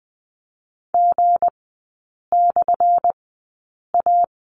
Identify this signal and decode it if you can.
Morse code, heard as a keyed tone, about 20 wpm, spaced as for 8 wpm, 700 Hz. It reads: G/A
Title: 178 Lesson 29e - Slash - Three Characters 20wpm